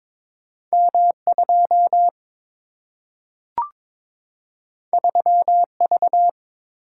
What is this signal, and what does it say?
Morse code, heard as a keyed tone, about 22 wpm, 700 Hz. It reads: M 2 3 V
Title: Single-Letter Single-Number 22wpm